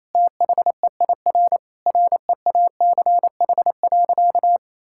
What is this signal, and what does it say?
Morse code, heard as a keyed tone, about 28 wpm, 700 Hz. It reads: THEIR REACH.
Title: Sentences from Top 1000 Words - Review Part 6 - 28wpm